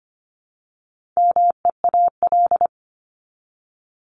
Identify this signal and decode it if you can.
Morse code, heard as a keyed tone, about 25 wpm, 700 Hz. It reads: MEAL
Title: Top 1751-2000 Words 25wpm